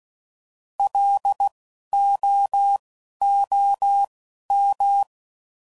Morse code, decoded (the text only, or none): LOOM